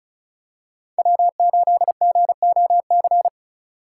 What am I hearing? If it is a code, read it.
Morse code, heard as a keyed tone, about 34 wpm, 700 Hz. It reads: W8GOC
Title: USA Call Signs - 6x Character Spacing - 35wpm